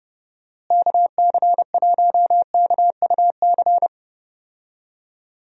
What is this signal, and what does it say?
Morse code, heard as a keyed tone, about 30 wpm, 700 Hz. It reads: KC1KUC
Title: USA Call Signs - 6 Characters - Repeated 3x - 30wpm